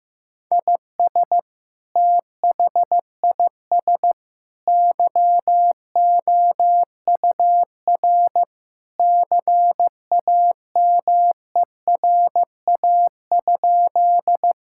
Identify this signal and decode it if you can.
Morse code, heard as a keyed tone, about 15 wpm, 700 Hz. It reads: IS THIS YOUR CAMERA?
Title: Sentences from Top 1000 Words - Review Part 5 - 15wpm